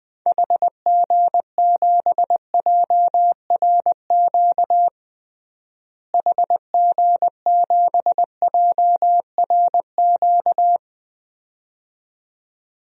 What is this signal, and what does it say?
Morse code, heard as a keyed tone, about 21 wpm, 700 Hz. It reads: HG7JRQ HG7JRQ
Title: DXCC-100 Call Signs - 6 Characters - Repeated 2x - 20wpm